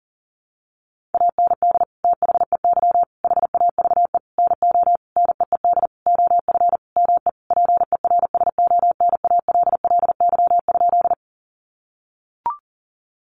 Morse code, heard as a keyed tone, about 40 wpm, 700 Hz. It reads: AND THEY HAVE NO NEED OF ME PERSONALLY?
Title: Sentences from Top 1400 Words 40wpm